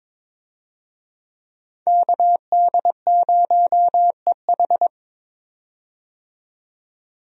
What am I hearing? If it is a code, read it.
Morse code, heard as a keyed tone, about 22 wpm, 700 Hz. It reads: KD0EH
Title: USA Call Signs - 5 Characters - 22wpm